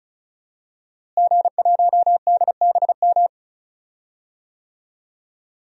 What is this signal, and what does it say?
Morse code, heard as a keyed tone, about 36 wpm, 700 Hz. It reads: G1DBM
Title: DXCC-100 Call Signs - 5 Characters - Repeated 3x - 35wpm